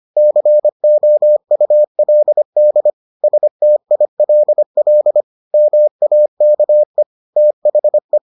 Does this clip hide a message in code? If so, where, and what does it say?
Yes: Morse code from 0.2 to 8.2 s, heard as a keyed tone, about 25 wpm, 600 Hz. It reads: COULD STILL MAKE THE